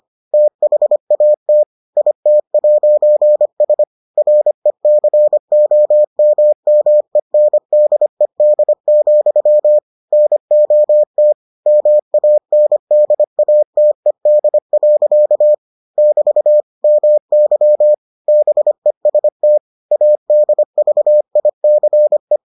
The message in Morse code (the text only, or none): THAT IT'S RECOMMENDED, NOT MANDATED. <BT> MY BEST ADVICE